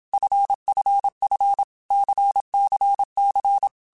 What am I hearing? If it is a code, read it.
Morse code, heard as a keyed tone, about 27 wpm, 800 Hz. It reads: FFF CCC
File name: Basic Letter Groups By7Groups / LFCYQ / LFCYQ@25